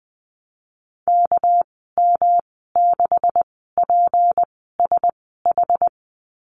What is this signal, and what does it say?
Morse code, heard as a keyed tone, about 20 wpm, spaced as for 14 wpm, 700 Hz. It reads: KM6PSH
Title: 200 Lesson 32g - Letter k - Call Signs 20wpm